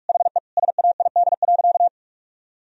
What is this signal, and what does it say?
Morse code, heard as a keyed tone, about 46 wpm, 700 Hz. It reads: HE SAID.
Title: Sentences from Top 1000 Words - Review Part 2 - 45wpm